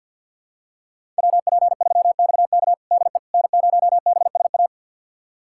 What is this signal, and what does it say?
Morse code, heard as a keyed tone, about 51 wpm, 700 Hz. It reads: WP3XK DE N0BSA
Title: USA Callsign DE Callsign - 5 Characters - 50wpm